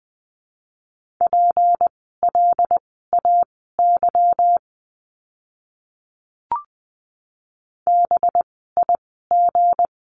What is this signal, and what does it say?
Morse code, heard as a keyed tone, about 20 wpm, spaced as for 14 wpm, 700 Hz. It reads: PLAY BIG
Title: 237 Lesson 38f - Letter q - Words 20wpm